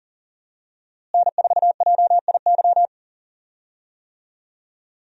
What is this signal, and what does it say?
Morse code, heard as a keyed tone, about 37 wpm, 700 Hz. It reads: N4JIY
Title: USA Call Signs - 5 Characters - Repeated 3x - 40wpm